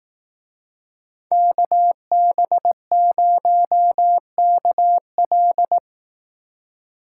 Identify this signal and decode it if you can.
Morse code, heard as a keyed tone, about 18 wpm, 700 Hz. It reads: KB0KL
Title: Speed Racing - USA Call Signs 15wpm